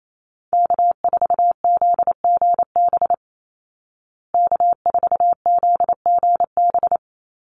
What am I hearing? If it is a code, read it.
Morse code, heard as a keyed tone, about 28 wpm, 700 Hz. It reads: K4ZGB K4ZGB
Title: Call Signs Repeated 2x - 28wpm